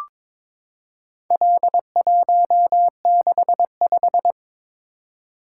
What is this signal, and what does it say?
Morse code, heard as a keyed tone, about 22 wpm, 700 Hz. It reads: L165
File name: practice / 22wpm / Four Letter-Number Combinations - 22wpm